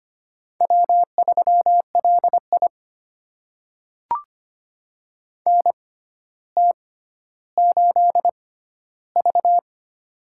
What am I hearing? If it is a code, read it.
Morse code, heard as a keyed tone, about 25 wpm, spaced as for 13 wpm, 700 Hz. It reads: W3LI NT8V
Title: USA Call Signs - 4 Characters - 6x Character Spacing - 25wpm